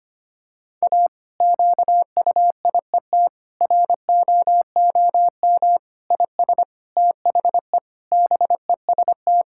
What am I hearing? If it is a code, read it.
Morse code, heard as a keyed tone, about 25 wpm, 700 Hz. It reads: A QUIET ROOM IS THE BEST
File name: practice / 25wpm / Difficult Sentences with Hints - 25wpm